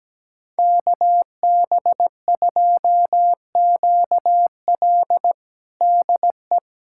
Morse code, heard as a keyed tone, about 17 wpm, 700 Hz. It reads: KB2QL DE
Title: USA Callsign DE Callsign - 5 Characters - 17wpm